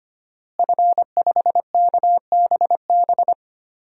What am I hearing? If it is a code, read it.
Morse code, heard as a keyed tone, about 25 wpm, 700 Hz. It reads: F5KBB
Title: DXCC-100 Call Signs - 5 Characters - Repeated 3x - 25wpm